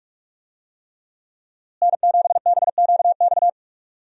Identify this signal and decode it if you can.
Morse code, heard as a keyed tone, about 46 wpm, 700 Hz. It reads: N7BQX
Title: USA Call Signs - 5 Characters - Repeated 3x - 45wpm